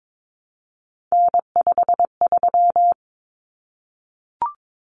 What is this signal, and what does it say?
Morse code, heard as a keyed tone, about 22 wpm, 700 Hz. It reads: N53
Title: Common Cut Numbers - 3-Digit 22wpm